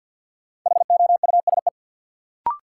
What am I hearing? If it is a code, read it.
Morse code, heard as a keyed tone, about 53 wpm, 700 Hz. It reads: HOUSE
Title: Sentences from Top 200 Words - New Word Drill - 50wpm